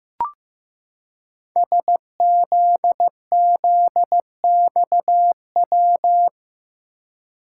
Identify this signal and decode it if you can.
Morse code, heard as a keyed tone, about 15 wpm, 700 Hz. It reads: SZZXW
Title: Code Groups - 5 Letters - 15wpm